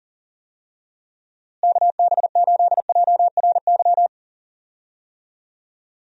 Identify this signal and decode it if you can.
Morse code, heard as a keyed tone, about 40 wpm, 700 Hz. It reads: KB8JRY
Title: USA Call Signs - 6 Characters - Repeated 2x - 40wpm